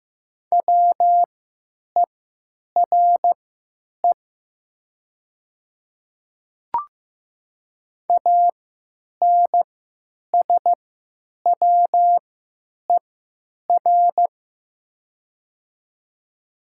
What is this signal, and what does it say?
Morse code, heard as a keyed tone, about 15 wpm, spaced as for 9 wpm, 700 Hz. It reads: WERE ANSWER
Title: 3x Character Spacing - Top 300 Words - 15wpm